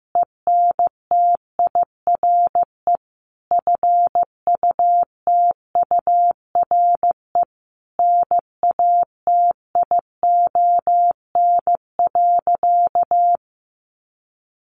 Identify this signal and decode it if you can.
Morse code, heard as a keyed tone, about 14 wpm, 700 Hz. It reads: ENTIRE FUTURE NATION.
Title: A Sound Of Thunder 15wpm